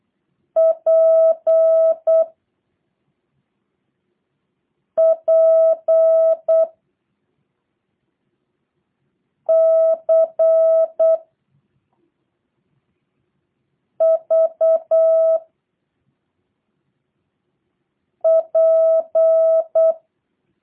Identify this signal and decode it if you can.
Morse code, heard as a keyed tone, about 8 wpm, 660 Hz. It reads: P P C V P
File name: jh8cbh / 2023cw / review1-5